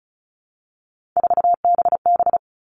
Morse code, heard as a keyed tone, about 34 wpm, 700 Hz. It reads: 4BB
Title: All Cut Numbers - 3-Digit 35wpm